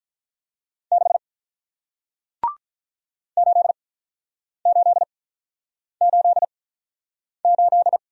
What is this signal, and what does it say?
Morse code, heard as a keyed tone, about 47 wpm, spaced as for 9 wpm, 700 Hz. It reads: 6 8888